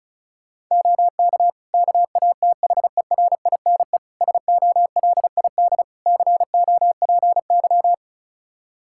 Morse code, heard as a keyed tone, about 36 wpm, 700 Hz. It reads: OK KATHERINE SOLID COPY
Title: Practice QSO - Exchange 3 - More Info - 35wpm